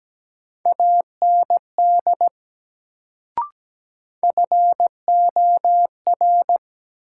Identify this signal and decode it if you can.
Morse code, heard as a keyed tone, about 17 wpm, 700 Hz. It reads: AND FOR
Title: Sentences from Top 100 Words - New Word Drill - 17wpm